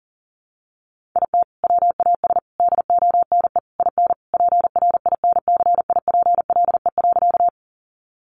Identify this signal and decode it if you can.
Morse code, heard as a keyed tone, about 40 wpm, 700 Hz. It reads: IT WAS DONE IN PRINCIPLE.
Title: Sentences from Top 1400 Words 40wpm